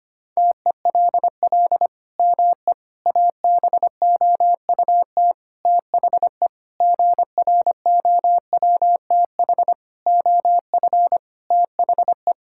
Morse code, heard as a keyed tone, about 25 wpm, 700 Hz. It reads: TELL ME ABOUT THE GROWTH OF THE
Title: Sentences from Top 1000 Words - Review Part 5 - 25wpm